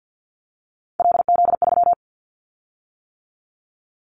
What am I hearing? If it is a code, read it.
Morse code, heard as a keyed tone, about 48 wpm, 700 Hz. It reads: LZ3
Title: Callsign Prefixes for Top 100 DXCC - 50wpm